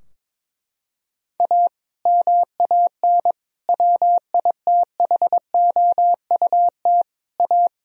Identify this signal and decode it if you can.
Morse code, heard as a keyed tone, about 22 wpm, 700 Hz. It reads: A MAN WITHOUT A
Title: Sentences from Top 1000 Words - Review Part 3 - 22wpm